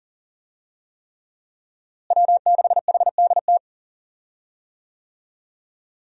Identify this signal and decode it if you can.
Morse code, heard as a keyed tone, about 40 wpm, 700 Hz. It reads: W6HDT
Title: USA Call Signs - 5 Characters - Repeated 3x - 40wpm